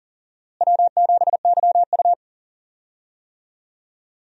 Morse code, heard as a keyed tone, about 40 wpm, 700 Hz. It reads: W7YU
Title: USA Call Signs - 4 Characters - Repeated 3x - 40wpm